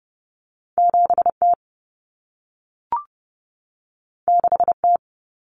Morse code, heard as a keyed tone, about 30 wpm, 700 Hz. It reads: 7T 6T
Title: Common Cut Numbers - 2-Digit 30wpm